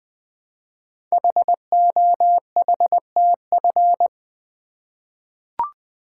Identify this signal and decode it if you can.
Morse code, heard as a keyed tone, about 20 wpm, 700 Hz. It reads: HOHTF
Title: Code Groups - 5 Letters - 20wpm